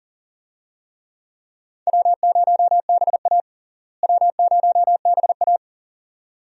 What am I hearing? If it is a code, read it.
Morse code, heard as a keyed tone, about 37 wpm, 700 Hz. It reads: W0BA W0BA
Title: USA Call Signs - 4 Characters - Repeated 2x - 40wpm